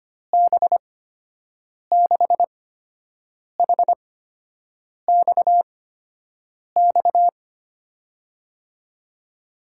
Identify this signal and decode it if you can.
Morse code, heard as a keyed tone, about 25 wpm, spaced as for 7 wpm, 700 Hz. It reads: B6HXX
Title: USA Call Signs - 6 Characters - 8x Character Spacing - 25wpm